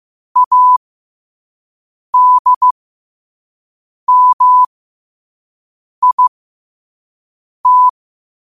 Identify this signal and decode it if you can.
Morse code, heard as a keyed tone, about 15 wpm, spaced as for 5 wpm, 1000 Hz. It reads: ADMIT
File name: qotdfiles / 5wpm